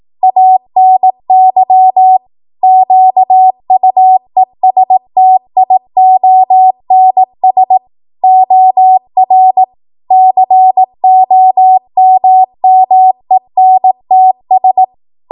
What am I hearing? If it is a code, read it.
Morse code, heard as a keyed tone, about 18 wpm, 750 Hz. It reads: ANY QUESTIONS OR COMMENTS
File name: Archive / 18 WPM / 18WPM